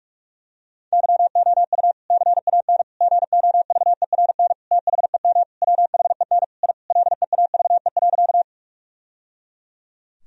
Morse code, heard as a keyed tone, about 46 wpm, 700 Hz. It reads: YOU CAN GOVERN THEM WHEN I LEAVE.
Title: Sentences from Top 1000 Words - Review Part 2 - 45wpm